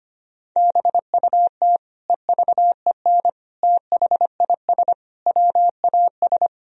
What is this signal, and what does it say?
Morse code, heard as a keyed tone, about 25 wpm, 700 Hz. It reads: BUT EVEN THIS WAS
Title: Sentences from Top 1000 Words - Review Part 2 - 25wpm